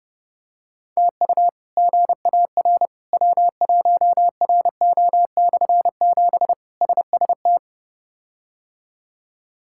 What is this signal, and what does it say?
Morse code, heard as a keyed tone, about 30 wpm, 700 Hz. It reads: TU GAR W1RO/7 SST
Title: K1USN SST QSO Practice - 30wpm